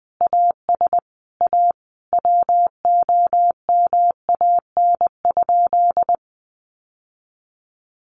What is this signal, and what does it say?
Morse code, heard as a keyed tone, about 20 wpm, 700 Hz. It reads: AS A WOMAN?
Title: Sentences from Top 1100 Words 20wpm